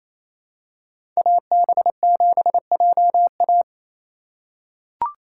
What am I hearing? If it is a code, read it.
Morse code, heard as a keyed tone, about 28 wpm, 700 Hz. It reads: AB7JA
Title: USA Call Signs - 5 Characters - 4x Character Spacing - 28wpm